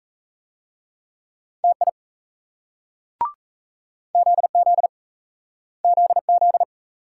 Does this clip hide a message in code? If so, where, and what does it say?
Yes: Morse code from 1.6 to 6.6 s, heard as a keyed tone, about 40 wpm, 700 Hz. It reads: TI ZZ ZZ